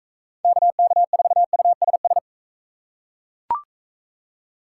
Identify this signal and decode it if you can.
Morse code, heard as a keyed tone, about 40 wpm, 700 Hz. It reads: KK4USS